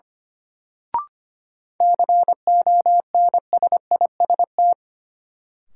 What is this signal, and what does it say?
Morse code, heard as a keyed tone, about 25 wpm, 700 Hz. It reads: CONSIST
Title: Sentences from Top 1400 Words - New Word Drill Alt - 25wpm